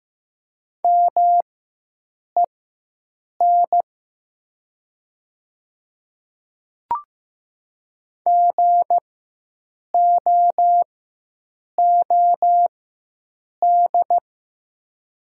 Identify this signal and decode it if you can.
Morse code, heard as a keyed tone, about 15 wpm, spaced as for 7 wpm, 700 Hz. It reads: MEN GOOD
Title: 4x Character Spacing - Top 200 Words - 15wpm